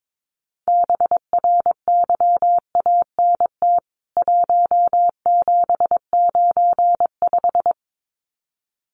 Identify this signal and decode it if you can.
Morse code, heard as a keyed tone, about 22 wpm, 700 Hz. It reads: BRYANT 1795
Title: CWT Exchange 22wpm